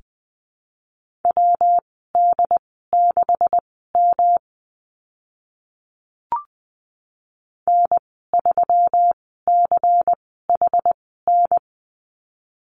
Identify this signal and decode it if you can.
Morse code, heard as a keyed tone, about 20 wpm, spaced as for 14 wpm, 700 Hz. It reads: WD6M N3CHN
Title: 154 Lesson 25g - Letter p - Call Signs 20wpm